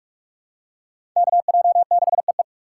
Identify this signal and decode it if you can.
Morse code, heard as a keyed tone, about 46 wpm, 700 Hz. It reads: KJ6EE
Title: USA Call Signs - 2x Character Spacing - 45wpm